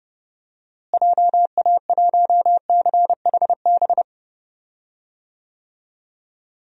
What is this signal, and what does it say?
Morse code, heard as a keyed tone, about 30 wpm, 700 Hz. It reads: JA1CHB
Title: DXCC-100 Call Signs - 6 Characters - Repeated 3x - 30wpm